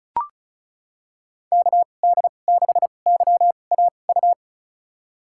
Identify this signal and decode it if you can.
Morse code, heard as a keyed tone, about 36 wpm, spaced as for 25 wpm, 700 Hz. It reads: KD6YAU
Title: USA Call Signs - 6 Characters - 2x Character Spacing - 35wpm